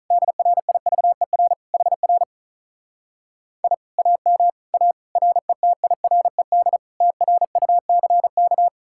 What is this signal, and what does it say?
Morse code, heard as a keyed tone, about 48 wpm, 700 Hz. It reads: DRIVER HR I AM A RETIRED TRUCK